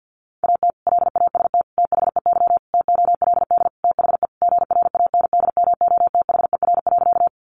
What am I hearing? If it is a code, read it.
Morse code, heard as a keyed tone, about 49 wpm, 700 Hz. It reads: AT LAST THEY TOLD THE GRANDMOTHER.
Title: Sentences from Top 1200 Words 50wpm